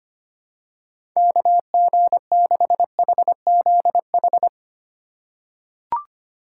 Morse code, heard as a keyed tone, about 25 wpm, 700 Hz. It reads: KG6HZH
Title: USA Call Signs - 6 Characters - 4x Character Spacing - 25wpm